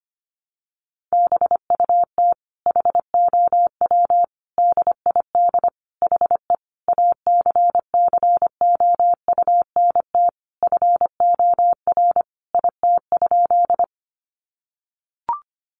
Morse code, heard as a keyed tone, about 25 wpm, 700 Hz. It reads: BUT HOW DID HE ACCOUNT FOR IT?
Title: Sentences from Top 900 Words 25wpm